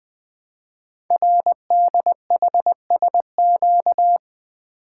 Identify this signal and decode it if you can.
Morse code, heard as a keyed tone, about 20 wpm, 700 Hz. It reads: RDHSQ
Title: Code Groups - 5 Letters - 20wpm